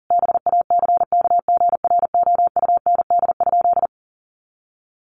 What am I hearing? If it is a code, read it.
Morse code, heard as a keyed tone, about 40 wpm, 700 Hz. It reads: BACKGROUND?
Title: Sentences from Top 1300 Words 40wpm